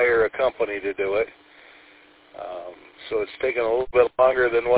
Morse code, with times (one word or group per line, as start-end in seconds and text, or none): none